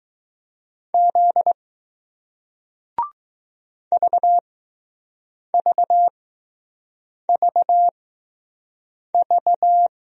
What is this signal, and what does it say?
Morse code, heard as a keyed tone, about 20 wpm, spaced as for 6 wpm, 700 Hz. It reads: Z VVVV